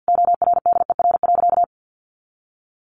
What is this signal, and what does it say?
Morse code, heard as a keyed tone, about 49 wpm, 700 Hz. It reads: ORDER.
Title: Sentences from Top 900 Words 50wpm